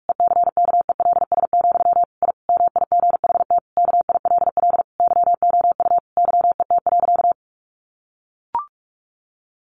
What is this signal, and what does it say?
Morse code, heard as a keyed tone, about 44 wpm, 700 Hz. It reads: ECKELS, I MIGHT KILL YOU YET.
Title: A Sound Of Thunder 45wpm